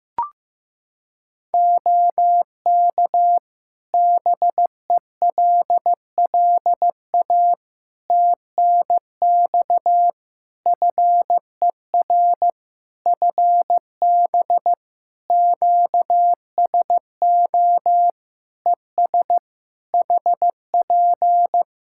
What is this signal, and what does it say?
Morse code, heard as a keyed tone, about 15 wpm, 700 Hz. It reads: OK BELLA TNX FER FB QSO ES HP
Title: K1USN SST QSO Practice 15wpm